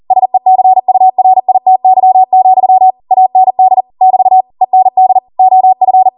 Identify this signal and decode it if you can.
Morse code, heard as a keyed tone, about 40 wpm, 750 Hz. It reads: SECURITY, AND <BT> END OF